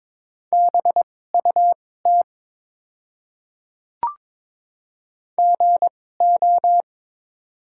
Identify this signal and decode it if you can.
Morse code, heard as a keyed tone, about 22 wpm, spaced as for 16 wpm, 700 Hz. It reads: BUT GO
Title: 2x Character Spacing - Top 200 Words - 22wpm